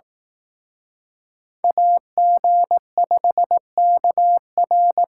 Morse code, heard as a keyed tone, about 18 wpm, 700 Hz. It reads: AG5KR
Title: Speed Racing - USA Call Signs 15wpm